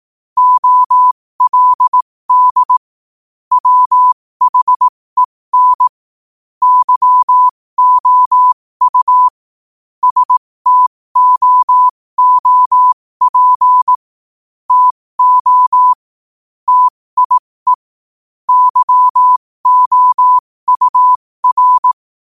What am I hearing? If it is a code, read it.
Morse code, heard as a keyed tone, about 18 wpm, spaced as for 15 wpm, 1000 Hz. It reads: OLD WHEN YOU STOOP TO TIE YOUR
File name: qotdfiles / 15wpm